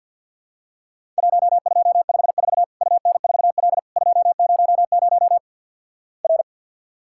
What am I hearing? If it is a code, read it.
Morse code, heard as a keyed tone, about 50 wpm, 700 Hz. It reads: 1254 UN4L 200 R
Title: Worked All Europe DX Contest - QTC Exchange with Repeats - 50wpm